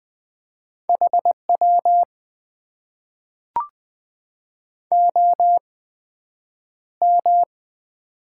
Morse code, heard as a keyed tone, about 20 wpm, spaced as for 9 wpm, 700 Hz. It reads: HW OM
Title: Top 90 QSO Elements - 8x Character Spacing - 20wpm